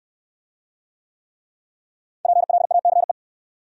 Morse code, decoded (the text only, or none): PLACE